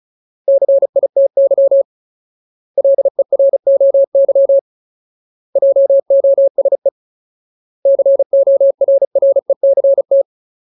Morse code, heard as a keyed tone, about 35 wpm, 550 Hz. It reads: CITY LEROY JOSE CORRECT
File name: Ultimate Word List / 35WPM Ultimate